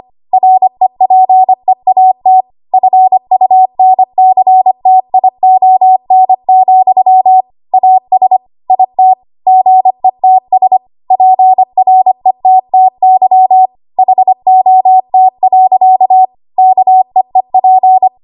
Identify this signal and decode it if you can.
Morse code, heard as a keyed tone, about 25 wpm, 750 Hz. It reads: REPEAT FUNCTION, AS IT GETS PRETTY HOT. KEEP
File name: Archive / 25 WPM / 25WPM